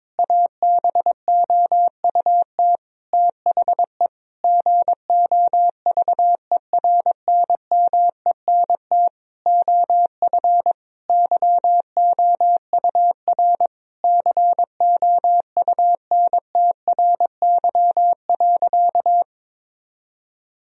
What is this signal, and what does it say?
Morse code, heard as a keyed tone, about 22 wpm, 700 Hz. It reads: ABOUT THE GOVERNMENT OF YOUR COUNTRY.
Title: Sentences from Top 1000 Words - Review Part 3 - 22wpm